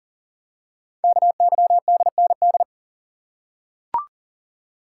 Morse code, heard as a keyed tone, about 44 wpm, 700 Hz. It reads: KYDND
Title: Code Groups - 5 Letters - 40wpm